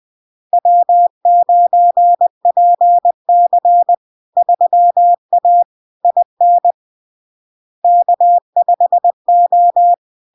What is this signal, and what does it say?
Morse code, heard as a keyed tone, about 20 wpm, 700 Hz. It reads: W9PC 3A IN K5O
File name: LARK-FD-2022-Log-in-CW-20wpm-700-Hz